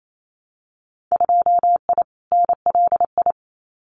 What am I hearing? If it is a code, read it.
Morse code, heard as a keyed tone, about 28 wpm, 700 Hz. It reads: 2I NLI
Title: Winter Field Day Exchanges 28wpm